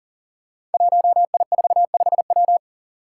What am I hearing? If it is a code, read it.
Morse code, heard as a keyed tone, about 38 wpm, 690 Hz. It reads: 1I45W
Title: Five Challenging Letter-Number Combinations 40wpm